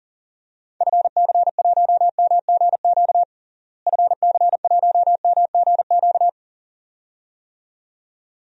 Morse code, heard as a keyed tone, about 37 wpm, 700 Hz. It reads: FC1MGQ FC1MGQ
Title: DXCC-100 Call Signs - 6 Characters - Repeated 2x - 40wpm